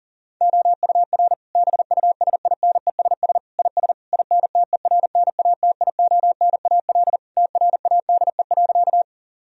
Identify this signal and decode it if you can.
Morse code, heard as a keyed tone, about 40 wpm, 700 Hz. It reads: OUR BUSINESS IS INTERNATIONAL TRADE.
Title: Sentences from Top 1000 Words - Review Part 4 - 40wpm